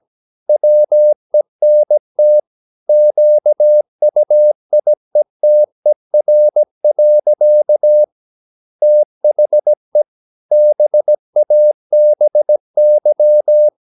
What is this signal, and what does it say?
Morse code, heard as a keyed tone, about 17 wpm, 600 Hz. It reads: WENT QUIETER. THE BABY
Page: Romanen Never af Ken Follet på engelsk.